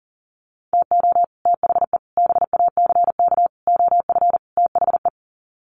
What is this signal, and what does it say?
Morse code, heard as a keyed tone, about 40 wpm, 700 Hz. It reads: TO THE BACK OF THE
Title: Sets of 5 Words 40wpm